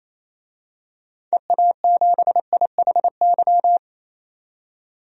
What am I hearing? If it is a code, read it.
Morse code, heard as a keyed tone, about 28 wpm, 700 Hz. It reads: EA7IHY
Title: DXCC-100 Call Signs - 6 Characters - 28wpm